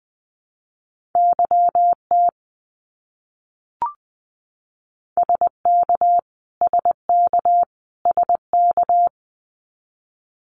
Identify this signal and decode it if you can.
Morse code, heard as a keyed tone, about 20 wpm, 700 Hz. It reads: YT SK SK SK
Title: Canadian Provinces and Territories - Repeated 3x - 20wpm